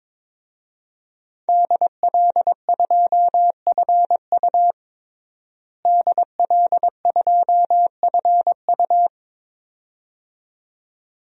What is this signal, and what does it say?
Morse code, heard as a keyed tone, about 22 wpm, 700 Hz. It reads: DL2FU DL2FU
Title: DXCC-100 Call Signs - 5 Characters - Repeated 2x - 22wpm